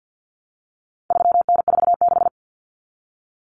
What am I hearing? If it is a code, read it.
Morse code, heard as a keyed tone, about 50 wpm, 700 Hz. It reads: 3N46
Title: Common Cut Numbers - 4-Digit 50wpm